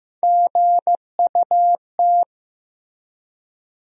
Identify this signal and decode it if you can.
Morse code, heard as a keyed tone, about 15 wpm, 700 Hz. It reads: GUT
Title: USA Call Sign Postfixes - 3 Letters 15wpm